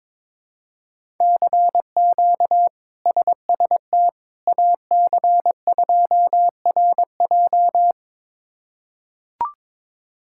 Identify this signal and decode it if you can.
Morse code, heard as a keyed tone, about 22 wpm, 700 Hz. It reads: CQ SST AC2RJ
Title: K1USN SST QSO Practice - 22wpm